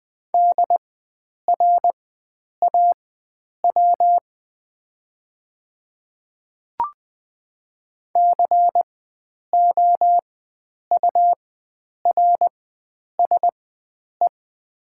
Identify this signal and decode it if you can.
Morse code, heard as a keyed tone, about 20 wpm, spaced as for 9 wpm, 700 Hz. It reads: DRAW COURSE
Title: 4x Character Spacing - Top 500 Words - 20wpm